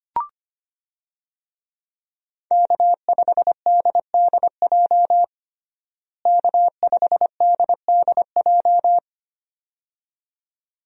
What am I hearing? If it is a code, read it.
Morse code, heard as a keyed tone, about 25 wpm, 700 Hz. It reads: K5DDJ K5DDJ
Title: USA Call Signs - 5 Characters - Repeated 2x - 25wpm